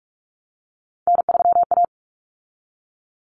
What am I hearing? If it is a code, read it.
Morse code, heard as a keyed tone, about 44 wpm, 700 Hz. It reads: N3A
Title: Common Cut Numbers - 3-Digit 45wpm